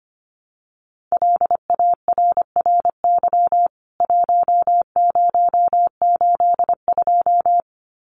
Morse code, heard as a keyed tone, about 25 wpm, 700 Hz. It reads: LARRY 1082
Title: CWT Exchange 25wpm